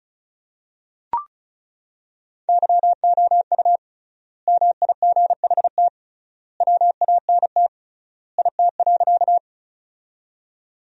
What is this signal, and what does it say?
Morse code, heard as a keyed tone, about 34 wpm, 700 Hz. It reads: YOU MIGHT WANT IT.
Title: 3x Word-Spacing - Sentences from Top 300 Words - 35wpm